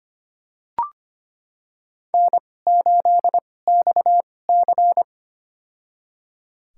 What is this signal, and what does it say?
Morse code, heard as a keyed tone, about 25 wpm, spaced as for 18 wpm, 700 Hz. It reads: N8XC
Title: USA Call Signs - 2x Character Spacing - 25wpm